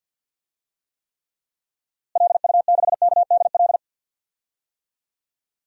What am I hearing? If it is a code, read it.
Morse code, heard as a keyed tone, about 48 wpm, 700 Hz. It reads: LU6KDL
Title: DXCC-100 Call Signs - 6 Characters - Repeated 3x - 50wpm